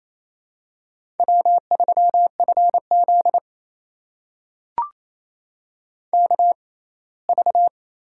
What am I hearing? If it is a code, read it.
Morse code, heard as a keyed tone, about 28 wpm, spaced as for 19 wpm, 700 Hz. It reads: W3FZ KV
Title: USA Call Signs - 4 Characters - 6x Character Spacing - 28wpm